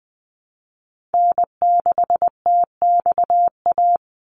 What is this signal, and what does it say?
Morse code, heard as a keyed tone, about 20 wpm, 700 Hz. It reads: N6TXA
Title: USA Call Signs 20wpm